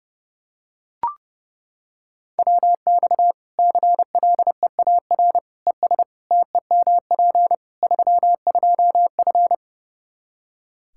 Morse code, heard as a keyed tone, about 30 wpm, 700 Hz. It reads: WX CLEAR ES TEMP 32F
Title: Practice QSO - Exchange 2 - Info - 30wpm